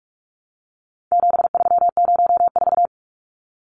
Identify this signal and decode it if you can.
Morse code, heard as a keyed tone, about 44 wpm, 700 Hz. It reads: 7304
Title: Numbers - 4-Digit 45wpm